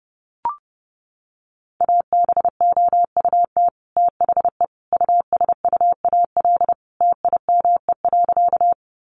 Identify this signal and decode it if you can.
Morse code, heard as a keyed tone, about 30 wpm, 700 Hz. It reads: ABOUT THE USUAL TIME.
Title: Sentences from Top 400 Words 30wpm